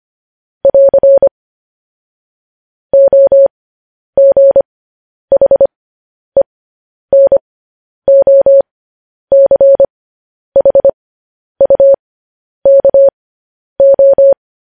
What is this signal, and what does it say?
Morse code, heard as a keyed tone, about 25 wpm, spaced as for 10 wpm, 550 Hz. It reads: <AR> OGHENOCHUKO
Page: news-headlines-in-morse-code-at-10-wpm